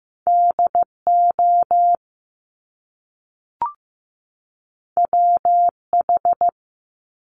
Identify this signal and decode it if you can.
Morse code, heard as a keyed tone, about 15 wpm, 700 Hz. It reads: DO WH
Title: Top Two Letter Combinations in Words 15wpm